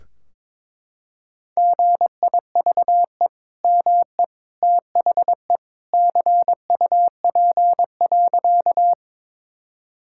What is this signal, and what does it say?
Morse code, heard as a keyed tone, about 22 wpm, 700 Hz. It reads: GIVE ME THE CUP.
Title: Sentences from Top 1600 Words 22wpm